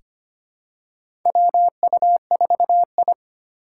Morse code, heard as a keyed tone, about 25 wpm, 700 Hz. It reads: WU4I
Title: USA Call Signs - 2x Character Spacing - 25wpm